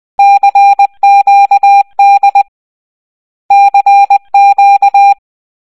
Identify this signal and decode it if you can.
Morse code, heard as a keyed tone, about 20 wpm, 800 Hz. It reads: CQD CQ
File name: raving-rabbids-travel-in-time-wii-gamerip-2010 / prmqdlitay / 9-06. Morse code